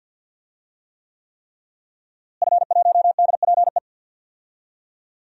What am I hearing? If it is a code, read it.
Morse code, heard as a keyed tone, about 52 wpm, 700 Hz. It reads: F1DPE
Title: DXCC-100 Call Signs - 5 Characters - Repeated 3x - 50wpm